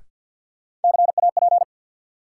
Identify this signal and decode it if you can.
Morse code, heard as a keyed tone, about 53 wpm, 700 Hz. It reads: CAP
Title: Sentences from Top 1600 Words - New Word Drill - 50wpm